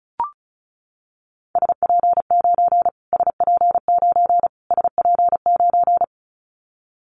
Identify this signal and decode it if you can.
Morse code, heard as a keyed tone, about 34 wpm, 700 Hz. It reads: SP9 SP9 SP9
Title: Callsign Prefixes for Top 25 DXCC - Repeated 3x - 35wpm